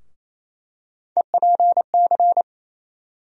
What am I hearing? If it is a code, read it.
Morse code, heard as a keyed tone, about 28 wpm, 700 Hz. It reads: EPC
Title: USA Call Sign Postfixes - 3 Letters 28wpm